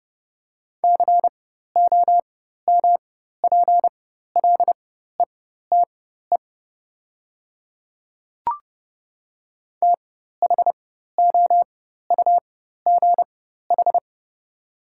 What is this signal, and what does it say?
Morse code, heard as a keyed tone, about 30 wpm, spaced as for 14 wpm, 700 Hz. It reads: COMPLETE THOUGH
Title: 4x Character Spacing - Top 500 Words - 30wpm